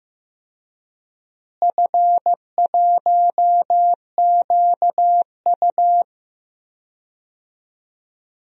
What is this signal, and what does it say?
Morse code, heard as a keyed tone, about 15 wpm, 700 Hz. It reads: F1QU
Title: DXCC-100 Call Signs - 4 Characters - Repeated 2x - 15wpm